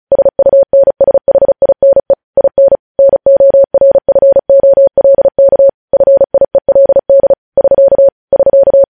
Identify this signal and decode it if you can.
Morse code, heard as a keyed tone, about 36 wpm, 550 Hz. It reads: SUNSHINE IN NORFOLK FIELD <SK> <SK>
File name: podcast / News headlines-1x-35x35